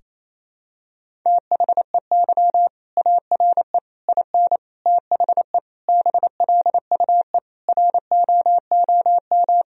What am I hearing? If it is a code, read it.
Morse code, heard as a keyed tone, about 28 wpm, 700 Hz. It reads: THEY ARE IN THE BLUE ROOM
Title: Sentences from Top 1000 Words - Review Part 3 - 28wpm